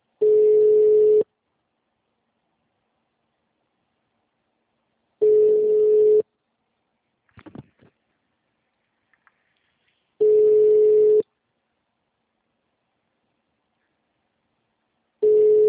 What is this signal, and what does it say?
Ringback tone (Europe, 425 Hz).